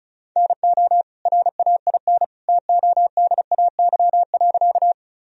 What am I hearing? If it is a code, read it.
Morse code, heard as a keyed tone, about 36 wpm, 700 Hz. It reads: NO RAIN TODAY.
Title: Sentences from Top 1000 Words - Review Part 1 - 35wpm